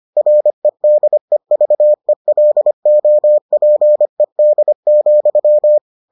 Morse code, heard as a keyed tone, about 25 wpm, 600 Hz. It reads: REDEVELOPED,